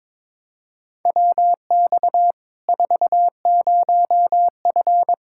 Morse code, heard as a keyed tone, about 22 wpm, 700 Hz. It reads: WX 40F
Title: Practice QSO - Exchange 2 - Info - 22wpm